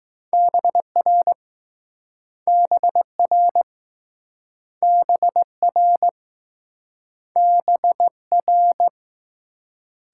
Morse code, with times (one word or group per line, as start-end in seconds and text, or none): 0.3-1.3 BR
2.5-3.6 BR
4.8-6.1 BR
7.4-8.1 B
8.3-8.9 R